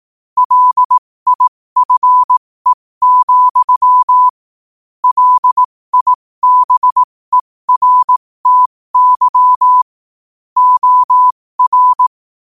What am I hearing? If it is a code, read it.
Morse code, heard as a keyed tone, about 18 wpm, spaced as for 15 wpm, 1000 Hz. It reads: LIFE, LIBERTY OR
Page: 15 WPM morse code quotes